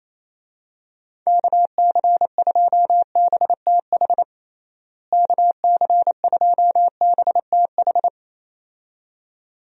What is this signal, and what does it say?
Morse code, heard as a keyed tone, about 28 wpm, 700 Hz. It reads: KC2BTH KC2BTH
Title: USA Call Signs - 6 Characters - Repeated 2x - 28wpm